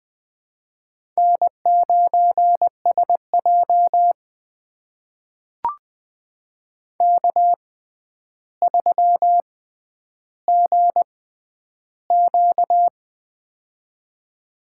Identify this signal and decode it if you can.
Morse code, heard as a keyed tone, about 20 wpm, spaced as for 10 wpm, 700 Hz. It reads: N9SJ K3GQ
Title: USA Call Signs - 4 Characters - 6x Character Spacing - 20wpm